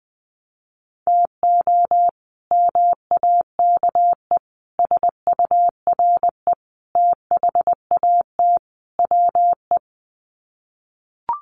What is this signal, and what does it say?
Morse code, heard as a keyed tone, about 20 wpm, 700 Hz. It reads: TO MAKE SURE THAT WE
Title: Sets of 5 Words Encore 20wpm